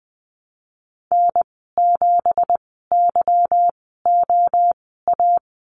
Morse code, heard as a keyed tone, about 20 wpm, spaced as for 15 wpm, 700 Hz. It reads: N7YOA
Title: 179 Lesson 29g - Slash - Call Signs 20wpm